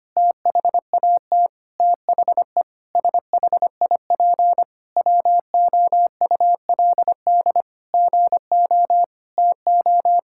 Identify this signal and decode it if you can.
Morse code, heard as a keyed tone, about 25 wpm, 700 Hz. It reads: THAT THE SHIP WOULD GO TO
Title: Sentences from Top 1000 Words - Review Part 6 - 25wpm